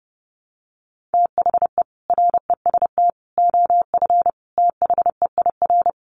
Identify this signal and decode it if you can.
Morse code, heard as a keyed tone, about 30 wpm, 700 Hz. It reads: THE REST OF THEIR
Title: Sets of 5 Words Encore 30wpm